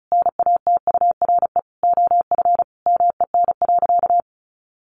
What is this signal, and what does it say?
Morse code, heard as a keyed tone, about 34 wpm, 700 Hz. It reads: NATURE OF MEN.
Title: Sentences from Top 1500 Words 35wpm